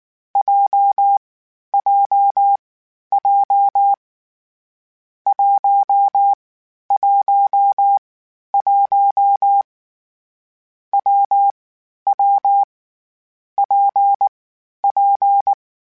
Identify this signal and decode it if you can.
Morse code, heard as a keyed tone, about 19 wpm, 800 Hz. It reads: J J J 1 1 1 W W P P
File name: Variable Speed / Variable Speed-mp3s / WPJ WPJ1 sp @18